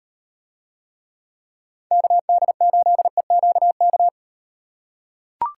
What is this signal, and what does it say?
Morse code, heard as a keyed tone, about 36 wpm, 700 Hz. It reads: KD8EQK